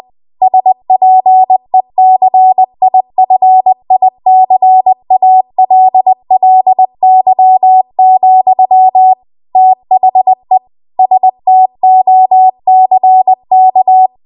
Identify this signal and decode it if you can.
Morse code, heard as a keyed tone, about 20 wpm, 750 Hz. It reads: SPECIFICALLY, THE STOCK
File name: Archive / 20 WPM / 20WPM